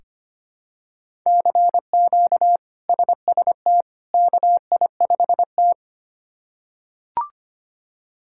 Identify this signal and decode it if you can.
Morse code, heard as a keyed tone, about 25 wpm, 700 Hz. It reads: CQ SST KI5T
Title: K1USN SST QSO Practice - 25wpm